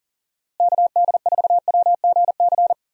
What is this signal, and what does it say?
Morse code, heard as a keyed tone, about 40 wpm, 700 Hz. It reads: KD4WGC
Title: USA Call Signs - 6 Characters - Repeated 3x - 40wpm